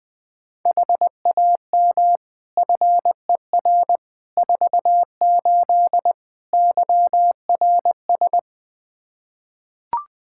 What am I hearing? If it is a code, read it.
Morse code, heard as a keyed tone, about 21 wpm, 700 Hz. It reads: HAM FER 48 YRS
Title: K1USN SST QSO Practice 20wpm